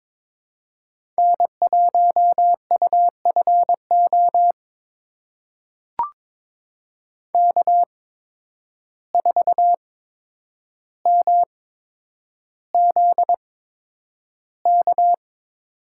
Morse code, heard as a keyed tone, about 22 wpm, spaced as for 9 wpm, 700 Hz. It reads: N1UFO K4MZK
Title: USA Call Signs - 5 Characters - 8x Character Spacing - 22wpm